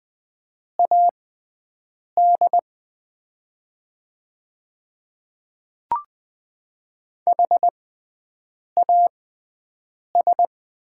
Morse code, heard as a keyed tone, about 20 wpm, spaced as for 7 wpm, 700 Hz. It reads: AD HAS
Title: 6x Character Spacing - Top 200 Words - 20wpm